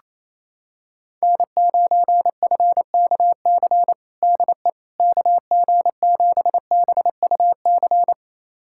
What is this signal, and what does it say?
Morse code, heard as a keyed tone, about 28 wpm, 700 Hz. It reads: N9FKC DE KG7BUC
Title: USA Callsign DE Callsign 28wpm